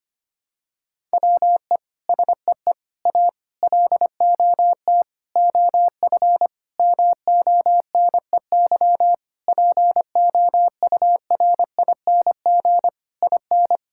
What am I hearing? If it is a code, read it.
Morse code, heard as a keyed tone, about 25 wpm, 700 Hz. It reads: WE SEE A LOT OF MONEY POURING IN
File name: practice / 25wpm / Sentences with 9-Word Phrases - 25wpm